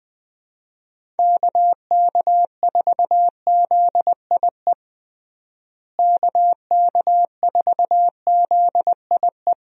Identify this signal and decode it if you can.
Morse code, heard as a keyed tone, about 21 wpm, 700 Hz. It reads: KK4ZIE KK4ZIE
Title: USA Call Signs - 6 Characters - Repeated 3x - 20wpm